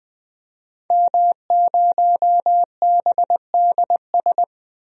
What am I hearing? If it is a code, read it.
Morse code, heard as a keyed tone, about 20 wpm, 700 Hz. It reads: M0BDS
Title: DXCC-100 Call Signs - 5 Characters - 20wpm